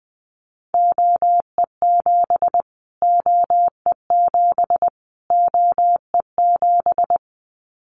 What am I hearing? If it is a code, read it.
Morse code, heard as a keyed tone, about 19 wpm, 700 Hz. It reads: OE7 OE7 OE7
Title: Callsign Prefixes for Top 25 DXCC - Repeated 3x - 20wpm